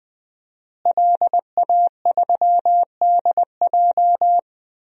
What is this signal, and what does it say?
Morse code, heard as a keyed tone, about 21 wpm, 700 Hz. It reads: LA3DJ
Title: DXCC-100 Call Signs - 5 Characters - Repeated 2x - 20wpm